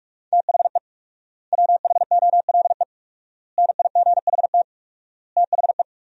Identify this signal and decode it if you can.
Morse code, heard as a keyed tone, about 42 wpm, 690 Hz. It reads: THE WHOLE NIGHT THE
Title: 4x Word-Spacing - Sentences from Top 500 Words - 45wpm